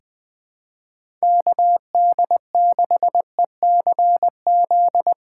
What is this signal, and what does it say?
Morse code, heard as a keyed tone, about 20 wpm, 700 Hz. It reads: KD6ECZ
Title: USA Call Signs - 6 Characters - 3x Character Spacing - 20wpm